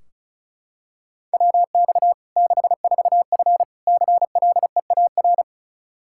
Morse code, heard as a keyed tone, about 36 wpm, 700 Hz. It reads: WX 64F CLEAR
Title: K1USN SST QSO Practice 35wpm